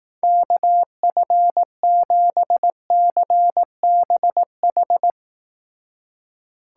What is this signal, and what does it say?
Morse code, heard as a keyed tone, about 18 wpm, 700 Hz. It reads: KF7CBH
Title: Speed Racing - USA Call Signs 15wpm